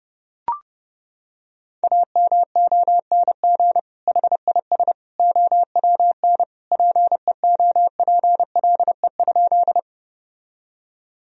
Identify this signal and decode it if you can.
Morse code, heard as a keyed tone, about 31 wpm, 700 Hz. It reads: AMONG HIS OWN PEOPLE?
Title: Sentences from Top 1000 Words - Review Part 3 - 30wpm